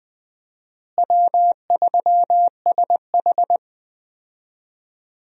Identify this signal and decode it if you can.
Morse code, heard as a keyed tone, about 20 wpm, 700 Hz. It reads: W3SH
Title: USA Call Signs - 4 Characters - Repeated 3x - 20wpm